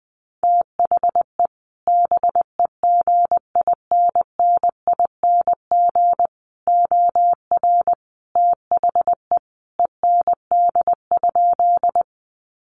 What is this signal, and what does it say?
Morse code, heard as a keyed tone, about 20 wpm, 700 Hz. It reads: THE BEGINNING OR THE END?
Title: Sentences from Top 800 Words 20wpm